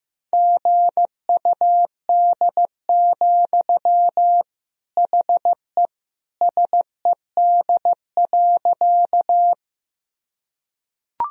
Morse code, heard as a keyed tone, about 15 wpm, 700 Hz. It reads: GUD, HE SED.
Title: Sentences from Top 200 Words and Abbreviations 15wpm